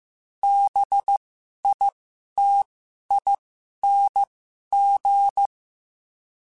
Morse code, heard as a keyed tone, about 15 wpm, spaced as for 11 wpm, 800 Hz. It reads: BITING
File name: Special INGs-mp3s / Special INGs-mp3s / Sngl Ing @10